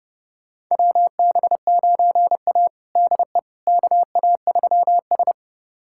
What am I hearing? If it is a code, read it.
Morse code, heard as a keyed tone, about 29 wpm, 700 Hz. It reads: WB9A DE KA3S
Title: USA Callsign DE Callsign - 4 Characters - 30wpm